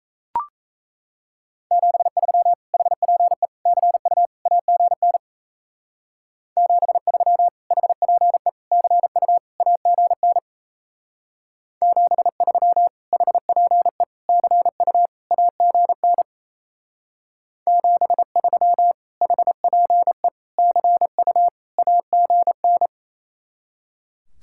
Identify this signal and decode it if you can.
Morse code, heard as a keyed tone, about 36 wpm, 700 Hz. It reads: 73 HPE CU AGN 73 HPE CU AGN 73 HPE CU AGN 73 HPE CU AGN